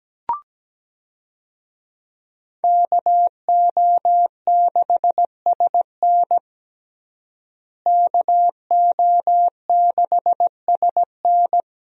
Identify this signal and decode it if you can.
Morse code, heard as a keyed tone, about 17 wpm, 700 Hz. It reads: KO6SN KO6SN
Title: USA Call Signs - 5 Characters - Repeated 3x - 17wpm